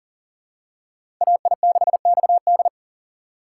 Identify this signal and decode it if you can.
Morse code, heard as a keyed tone, about 40 wpm, 700 Hz. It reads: AI6XD
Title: USA Call Signs - 5 Characters - 2x Character Spacing - 40wpm